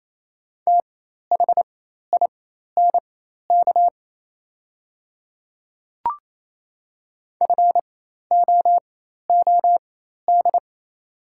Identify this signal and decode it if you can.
Morse code, heard as a keyed tone, about 28 wpm, spaced as for 13 wpm, 700 Hz. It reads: THINK FOOD
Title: 4x Character Spacing - Top 300 Words - 28wpm